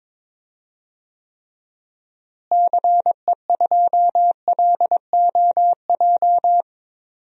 Morse code, heard as a keyed tone, about 22 wpm, 700 Hz. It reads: CE2LOJ
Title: DXCC-100 Call Signs - 6 Characters - Repeated 3x - 22wpm